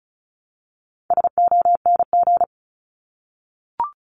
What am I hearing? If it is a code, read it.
Morse code, heard as a keyed tone, about 34 wpm, 700 Hz. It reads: SONG
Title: Top 500 Words Repeated 35wpm